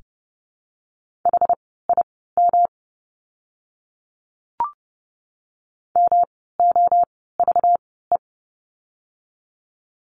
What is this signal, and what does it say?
Morse code, heard as a keyed tone, about 30 wpm, spaced as for 17 wpm, 700 Hz. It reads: HIM MOVE
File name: course / 30:10wpm / 237 Lesson 38f - Letter q - Words 30wpm